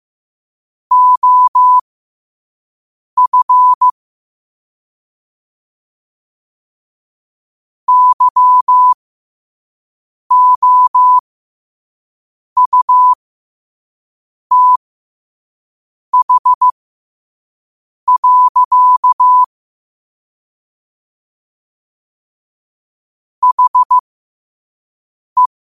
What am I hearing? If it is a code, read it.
Morse code, heard as a keyed tone, about 15 wpm, spaced as for 5 wpm, 1000 Hz. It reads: OF YOUTH. HE
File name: qotdfiles / 5wpm